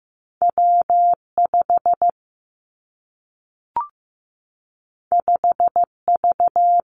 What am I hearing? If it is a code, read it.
Morse code, heard as a keyed tone, about 15 wpm, 700 Hz. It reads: W 5 5 V
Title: Two Challenging Letter-Number Combinations 15wpm